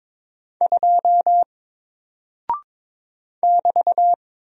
Morse code, heard as a keyed tone, about 22 wpm, 700 Hz. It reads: 2 <BT>
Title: Single Character (Letter:Number:Punctuation:Prosign) 22wpm